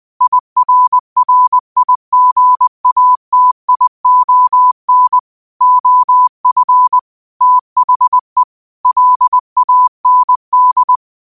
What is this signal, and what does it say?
Morse code, heard as a keyed tone, about 20 wpm, 1000 Hz. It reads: IRRIGATION OF THE LAND